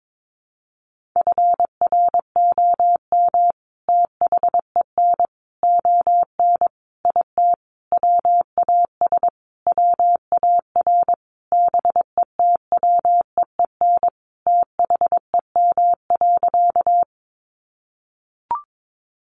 Morse code, heard as a keyed tone, about 22 wpm, 700 Hz. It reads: FROM THEN ON IT WAS WAR BETWEEN THEM.
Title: Sentences from Top 400 Words 22wpm